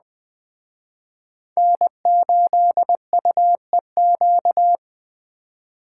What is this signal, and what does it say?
Morse code, heard as a keyed tone, about 20 wpm, 700 Hz. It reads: N8UEQ
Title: Speed Racing - USA 1x3 Call Signs 17wpm